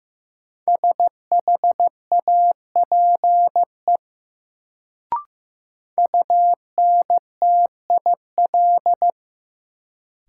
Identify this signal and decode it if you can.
Morse code, heard as a keyed tone, about 15 wpm, 700 Hz. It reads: SHAPE UNTIL
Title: Sentences from Top 500 Words - New Word Drill - 15wpm